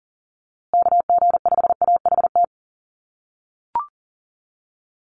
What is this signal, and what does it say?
Morse code, heard as a keyed tone, about 40 wpm, 700 Hz. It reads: KG5AHT
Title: USA Call Signs Repeated 3x 40wpm